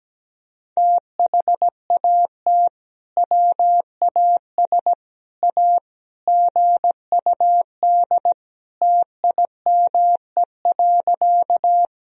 Morse code, heard as a keyed tone, about 17 wpm, 700 Hz. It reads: THAT WAS A GUD TIME.
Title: Sentences from Top 200 Words and Abbreviations 17wpm